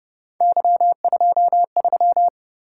Morse code, heard as a keyed tone, about 29 wpm, 700 Hz. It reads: Y23
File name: Speed-Racing - Three Letter-Number Combinations - 25wpm